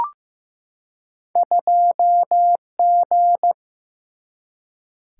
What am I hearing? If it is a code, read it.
Morse code, heard as a keyed tone, about 15 wpm, 700 Hz. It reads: 2G
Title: Single-Letter Single-Number 15wpm